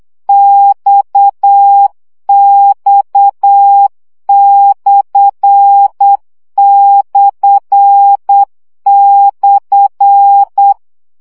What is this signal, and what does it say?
Morse code, heard as a keyed tone, about 8 wpm, 800 Hz. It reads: XX///
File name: CW gyakorlas x per 7 baud